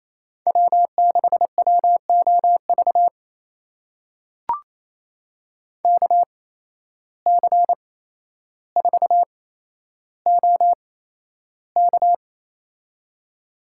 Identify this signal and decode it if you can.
Morse code, heard as a keyed tone, about 28 wpm, spaced as for 12 wpm, 700 Hz. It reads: W6WOV KC4OK
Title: USA Call Signs - 5 Characters - 8x Character Spacing - 28wpm